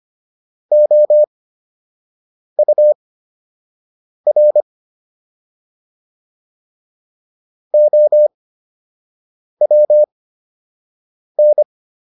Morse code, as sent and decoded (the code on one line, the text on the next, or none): --- ..- .-. / --- .-- -.
OUR OWN